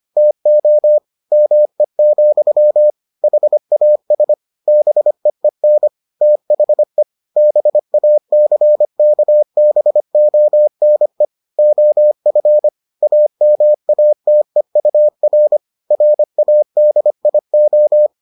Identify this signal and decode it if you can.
Morse code, heard as a keyed tone, about 25 wpm, 600 Hz. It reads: TO ME, HAS BEEN THE BACKBONE OF AMATEUR RADIO